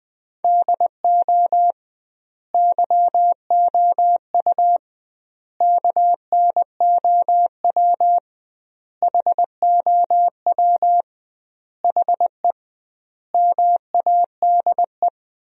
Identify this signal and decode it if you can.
Morse code, heard as a keyed tone, about 21 wpm, 700 Hz. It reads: DO YOU KNOW HOW HE MADE
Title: 2x Word-Spacing - Sentences from Top 300 Words - 20wpm